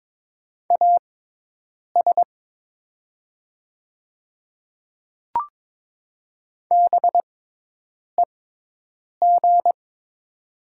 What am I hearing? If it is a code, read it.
Morse code, heard as a keyed tone, about 22 wpm, spaced as for 8 wpm, 700 Hz. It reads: AS BEG
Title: 6x Character Spacing - Top 500 Words - 22wpm